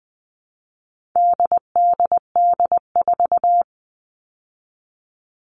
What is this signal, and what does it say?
Morse code, heard as a keyed tone, about 20 wpm, 700 Hz. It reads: DDD4
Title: All Cut Numbers - 4-Digit 20wpm